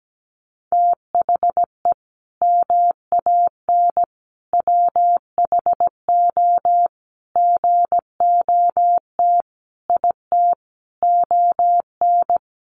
Morse code, heard as a keyed tone, about 17 wpm, 700 Hz. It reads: THE MAN WHO GOT IT ON
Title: Sentences from Top 1400 Words 17wpm